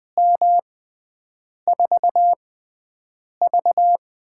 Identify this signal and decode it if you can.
Morse code, heard as a keyed tone, about 20 wpm, spaced as for 7 wpm, 700 Hz. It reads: M4V
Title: USA Call Signs - 4 Characters - 6x Character Spacing - 20wpm